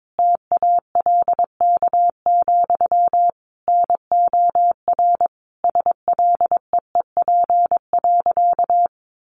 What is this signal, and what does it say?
Morse code, heard as a keyed tone, about 22 wpm, 700 Hz. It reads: TALK, NOR SLEEP.
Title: Sentences from Top 1500 Words 22wpm